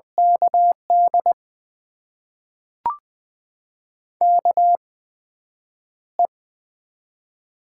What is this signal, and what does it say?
Morse code, heard as a keyed tone, about 20 wpm, spaced as for 9 wpm, 700 Hz. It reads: KD KE
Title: USA Call Signs - 6 Characters - 8x Character Spacing - 20wpm